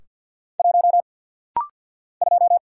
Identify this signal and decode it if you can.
Morse code, heard as a keyed tone, about 48 wpm, 700 Hz. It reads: 1 2
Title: Single Letter-Number - Rapid-Fire 50wpm